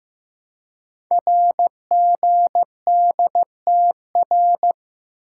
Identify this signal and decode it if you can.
Morse code, heard as a keyed tone, about 15 wpm, 700 Hz. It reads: RGDTR
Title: Code Groups - 5 Letters - 15wpm